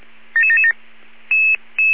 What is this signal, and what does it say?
Five-tone selective call (ZVEI), heard as quick runs of five tones, tones of about 70 ms. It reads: 77777